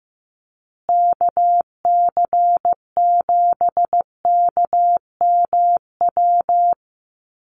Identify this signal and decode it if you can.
Morse code, heard as a keyed tone, about 15 wpm, 700 Hz. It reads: KC7KMW
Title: USA Call Signs Repeated 3x 15wpm